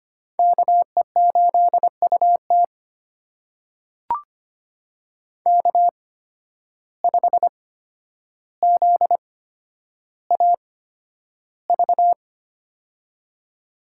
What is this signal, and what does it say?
Morse code, heard as a keyed tone, about 25 wpm, spaced as for 11 wpm, 700 Hz. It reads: KE8UT K5ZAV
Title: USA Call Signs - 5 Characters - 8x Character Spacing - 25wpm